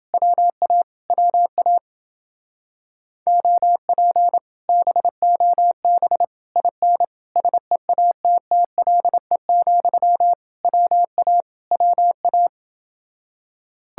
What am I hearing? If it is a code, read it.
Morse code, heard as a keyed tone, about 29 wpm, 700 Hz. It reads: WA WA OP BOB IN SEATTLE, WA WA